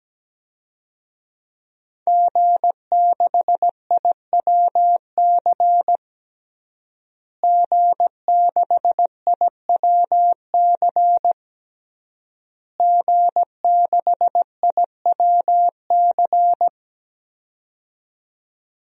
Morse code, heard as a keyed tone, about 17 wpm, 700 Hz. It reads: G6IWC G6IWC G6IWC
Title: DXCC-100 Call Signs - 5 Characters - Repeated 3x - 17wpm